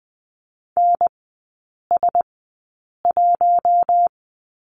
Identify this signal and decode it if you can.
Morse code, heard as a keyed tone, about 20 wpm, spaced as for 8 wpm, 700 Hz. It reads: NS1
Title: 041 Lesson 8e - Number 1 - Three Characters 20wpm